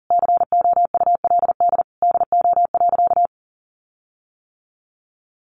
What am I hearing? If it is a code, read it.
Morse code, heard as a keyed tone, about 40 wpm, 700 Hz. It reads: COULD DO.
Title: Sentences from Top 1400 Words 40wpm